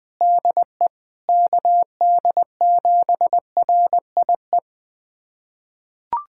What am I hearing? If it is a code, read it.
Morse code, heard as a keyed tone, about 20 wpm, 700 Hz. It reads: DE KD7RIE
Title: USA Callsign DE Callsign - 6 Characters - 20wpm